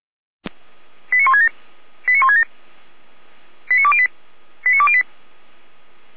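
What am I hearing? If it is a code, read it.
Five-tone selective call (ZVEI), heard as quick runs of five tones, tones of about 70 ms. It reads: 89167 89167 89208 89208
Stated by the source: ZVEI five-tone selective call received on an SDR.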